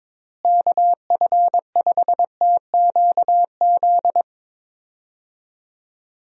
Morse code, heard as a keyed tone, about 22 wpm, 700 Hz. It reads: KF5TQZ
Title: USA Call Signs - 6 Characters - Repeated 2x - 22wpm